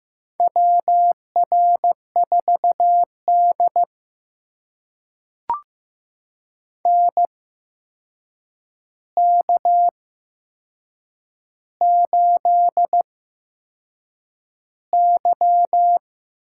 Morse code, heard as a keyed tone, about 15 wpm, spaced as for 6 wpm, 700 Hz. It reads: WR4D NK8Y
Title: USA Call Signs - 4 Characters - 8x Character Spacing - 15wpm